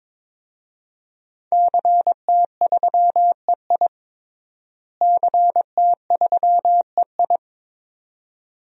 Morse code, heard as a keyed tone, about 22 wpm, 700 Hz. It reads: CT3EI CT3EI
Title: DXCC-100 Call Signs - 5 Characters - Repeated 2x - 22wpm